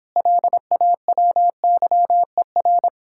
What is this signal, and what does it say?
Morse code, heard as a keyed tone, about 29 wpm, 700 Hz. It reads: LAWYER